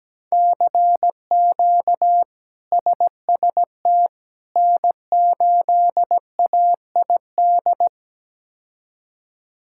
Morse code, heard as a keyed tone, about 17 wpm, 700 Hz. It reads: CQ SST N8AID
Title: K1USN SST QSO Practice - 17wpm